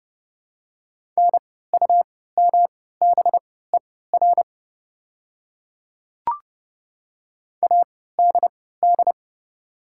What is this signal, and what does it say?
Morse code, heard as a keyed tone, about 31 wpm, spaced as for 17 wpm, 700 Hz. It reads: NUMBER ADD
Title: 3x Character Spacing - Top 200 Words - 30wpm